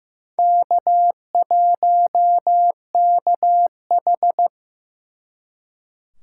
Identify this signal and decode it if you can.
Morse code, heard as a keyed tone, about 18 wpm, 700 Hz. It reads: K1KH
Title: Speed Racing - USA 1x2 Call Signs 15wpm